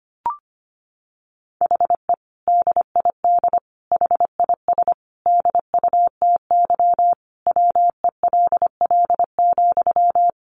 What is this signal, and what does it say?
Morse code, heard as a keyed tone, about 25 wpm, 700 Hz. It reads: HE DID HIS DUTY WELL,
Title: Sentences from Top 1500 Words 25wpm